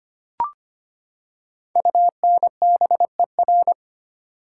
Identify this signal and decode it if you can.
Morse code, heard as a keyed tone, about 25 wpm, 700 Hz. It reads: UNBER
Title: Code Groups - 5 Letters - 25wpm